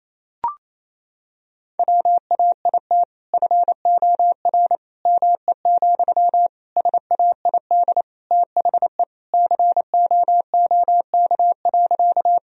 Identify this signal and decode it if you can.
Morse code, heard as a keyed tone, about 28 wpm, 700 Hz. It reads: WAIT FOR ME, SAID THE COOK.
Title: Sentences from Top 1000 Words - Review Part 4 - 28wpm